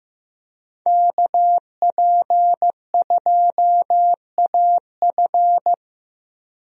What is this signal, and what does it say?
Morse code, heard as a keyed tone, about 15 wpm, 700 Hz. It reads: KP2AF
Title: USA Call Signs - 5 Characters - 15wpm